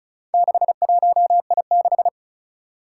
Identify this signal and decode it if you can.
Morse code, heard as a keyed tone, about 36 wpm, 700 Hz. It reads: 61I6
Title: Four Challenging Letter-Number Combinations 35wpm